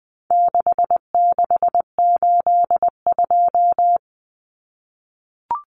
Morse code, heard as a keyed tone, about 19 wpm, 700 Hz. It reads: 6682
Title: Numbers - 4-Digit 20wpm